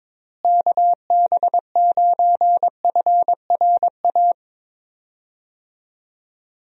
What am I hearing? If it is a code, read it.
Morse code, heard as a keyed tone, about 22 wpm, 700 Hz. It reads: KB9FRA
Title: USA Call Signs - 6 Characters - Repeated 3x - 22wpm